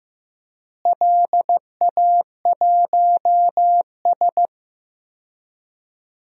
Morse code, heard as a keyed tone, about 15 wpm, 700 Hz. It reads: LA1S
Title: DXCC-100 Call Signs - 4 Characters - Repeated 2x - 15wpm